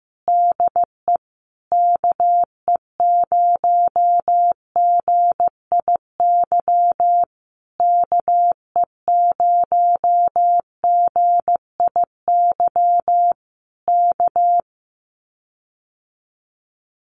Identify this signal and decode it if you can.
Morse code, heard as a keyed tone, about 15 wpm, 700 Hz. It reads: DE KE0GIY KE0GIY K
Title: Common QSO Phrases Encore 15wpm